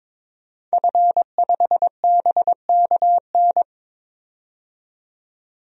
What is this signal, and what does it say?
Morse code, heard as a keyed tone, about 22 wpm, 700 Hz. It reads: F5BKN
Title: DXCC-100 Call Signs - 5 Characters - 22wpm